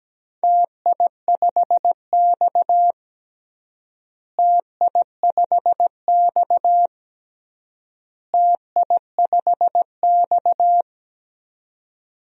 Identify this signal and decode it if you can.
Morse code, heard as a keyed tone, about 17 wpm, 700 Hz. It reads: TI5X TI5X TI5X
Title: DXCC-100 Call Signs - 4 Characters - Repeated 3x - 17wpm